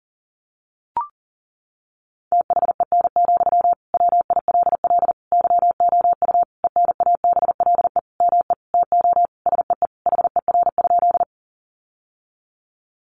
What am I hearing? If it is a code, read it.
Morse code, heard as a keyed tone, about 40 wpm, 700 Hz. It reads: THEN, WILL YOU ENABLE ME TO SEE HER?
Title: Sentences from Top 1500 Words 40wpm